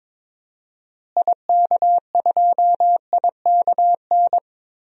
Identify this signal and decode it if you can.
Morse code, heard as a keyed tone, about 22 wpm, 700 Hz. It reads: IK2IKN
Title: DXCC-100 Call Signs - 6 Characters - Repeated 3x - 22wpm